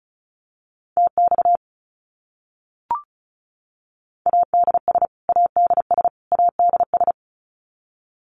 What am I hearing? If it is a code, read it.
Morse code, heard as a keyed tone, about 34 wpm, 700 Hz. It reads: TX ADS ADS ADS
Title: Abbreviations Repeated 3x - 35wpm